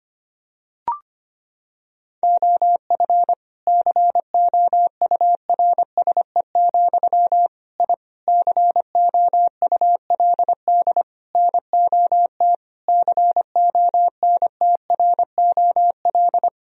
Morse code, heard as a keyed tone, about 25 wpm, 700 Hz. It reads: OF COURSE, I COULD NOT CONTROL
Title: Sentences from Top 1000 Words - Review Part 3 - 25wpm